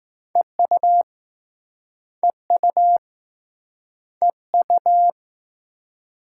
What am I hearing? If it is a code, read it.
Morse code, heard as a keyed tone, about 20 wpm, 700 Hz. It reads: EU EU EU